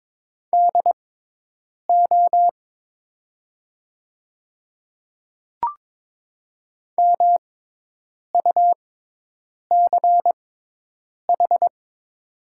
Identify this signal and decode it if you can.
Morse code, heard as a keyed tone, about 22 wpm, spaced as for 8 wpm, 700 Hz. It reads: DO MUCH
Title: 6x Character Spacing - Top 200 Words - 22wpm